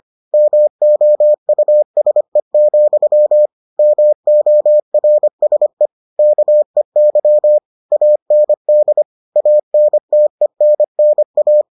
Morse code, heard as a keyed tone, about 25 wpm, 600 Hz. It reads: MOUSE, MORSE KEY AND ANTENNA